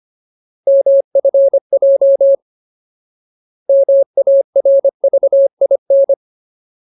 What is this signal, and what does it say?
Morse code, heard as a keyed tone, about 25 wpm, 550 Hz. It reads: MFJ MARVIN
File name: Ultimate Word List / 25WPM Ultimate